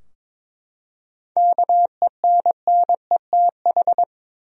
Morse code, heard as a keyed tone, about 22 wpm, 700 Hz. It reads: KENNETH
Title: USA Common Names - 22wpm